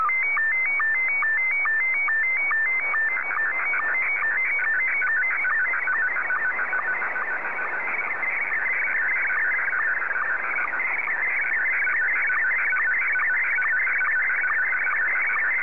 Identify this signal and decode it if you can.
SSTV (slow-scan television), Scottie 1.